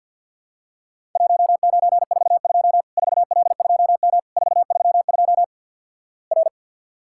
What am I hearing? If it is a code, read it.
Morse code, heard as a keyed tone, about 50 wpm, 700 Hz. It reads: 1942 4L2M 432 R
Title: Worked All Europe DX Contest - QTC Exchange with Repeats - 50wpm